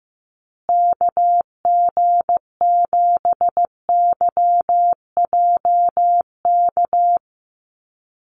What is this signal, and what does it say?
Morse code, heard as a keyed tone, about 15 wpm, 700 Hz. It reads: KG7YJK
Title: USA Call Signs Repeated 3x 15wpm